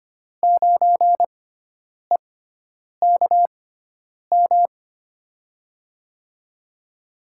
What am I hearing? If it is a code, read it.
Morse code, heard as a keyed tone, about 25 wpm, spaced as for 9 wpm, 700 Hz. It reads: 9EKM
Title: USA Call Signs - 6 Characters - 6x Character Spacing - 25wpm